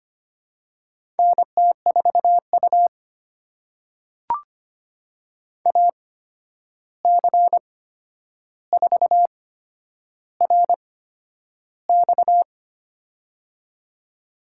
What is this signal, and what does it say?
Morse code, heard as a keyed tone, about 25 wpm, spaced as for 10 wpm, 700 Hz. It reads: NT4U AC4RX
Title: USA Call Signs - 8x Character Spacing - 25wpm